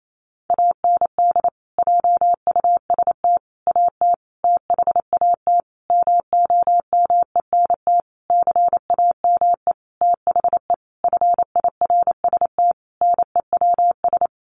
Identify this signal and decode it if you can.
Morse code, heard as a keyed tone, about 28 wpm, 700 Hz. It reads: AND JUST AT THAT MOMENT CAME THE FIRST NEWS
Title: Sentences from Top 1500 Words 28wpm